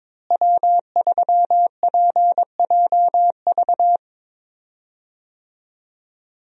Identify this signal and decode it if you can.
Morse code, heard as a keyed tone, about 22 wpm, 700 Hz. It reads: W3PJV
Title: USA Call Signs - 5 Characters - Repeated 3x - 22wpm